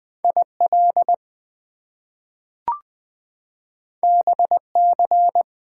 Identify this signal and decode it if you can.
Morse code, heard as a keyed tone, about 20 wpm, 700 Hz. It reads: IL BC
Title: US States and Canadian Provinces 20wpm